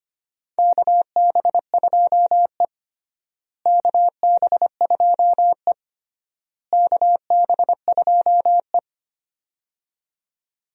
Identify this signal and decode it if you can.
Morse code, heard as a keyed tone, about 25 wpm, 700 Hz. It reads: KB2E KB2E KB2E
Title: USA Call Signs - 4 Characters - Repeated 3x - 25wpm